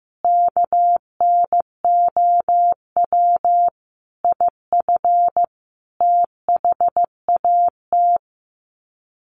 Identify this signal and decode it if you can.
Morse code, heard as a keyed tone, about 14 wpm, 700 Hz. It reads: KNOW IF THAT
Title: Sets of 5 Words Encore2 15wpm